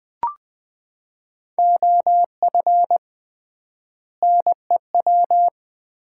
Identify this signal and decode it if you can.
Morse code, heard as a keyed tone, about 20 wpm, 700 Hz. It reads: OF NEW
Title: 3x Word-Spacing - Sets of 3 Words - 20wpm